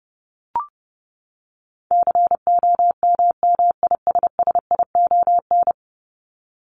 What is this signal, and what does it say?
Morse code, heard as a keyed tone, about 30 wpm, 700 Hz. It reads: COMMISSION
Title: Top 1251-1500 Words 30wpm